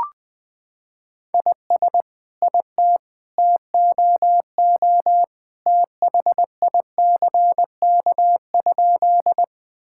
Morse code, heard as a keyed tone, about 20 wpm, 700 Hz. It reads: IS IT TOO THICK?
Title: Sentences from Top 1600 Words 20wpm